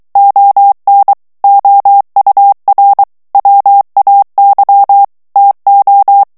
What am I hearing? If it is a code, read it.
Morse code, heard as a keyed tone, about 23 wpm, 800 Hz. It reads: ON OUR WAY TO
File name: morse mp3 / ANGLAIS texte clair 2 vitesse 22 mots minute